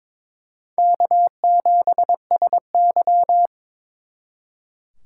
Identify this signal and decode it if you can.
Morse code, heard as a keyed tone, about 26 wpm, 700 Hz. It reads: K7SY
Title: Speed Racing - USA 1x2 Call Signs 22wpm